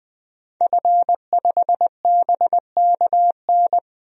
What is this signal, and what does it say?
Morse code, heard as a keyed tone, about 21 wpm, 700 Hz. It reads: F5BKN
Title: DXCC-100 Call Signs - 5 Characters - 20wpm